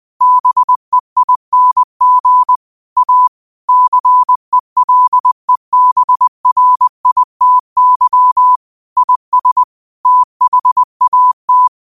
Morse code, heard as a keyed tone, about 20 wpm, 1000 Hz. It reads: BEING A CELEBRITY IS THAT
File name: qotdfiles / 20wpm